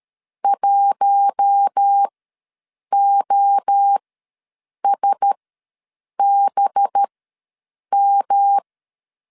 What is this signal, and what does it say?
Morse code, heard as a keyed tone, about 13 wpm, spaced as for 7 wpm, 800 Hz. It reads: 1OSBM
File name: 5 LB ws 12wpm